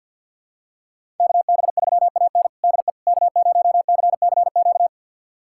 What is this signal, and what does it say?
Morse code, heard as a keyed tone, about 50 wpm, 700 Hz. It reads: KB3AN DE K0CCQ
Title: USA Callsign DE Callsign - 5 Characters - 50wpm